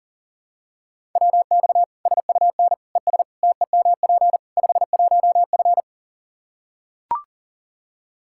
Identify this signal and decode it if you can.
Morse code, heard as a keyed tone, about 40 wpm, 700 Hz. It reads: WX SUN ES TEMP 51F
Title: Practice QSO - Exchange 2 - Info - 40wpm